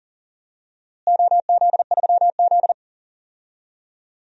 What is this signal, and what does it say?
Morse code, heard as a keyed tone, about 40 wpm, 690 Hz. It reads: OZ3Z
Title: DXCC-100 Call Signs - 4 Characters - Repeated 2x - 40wpm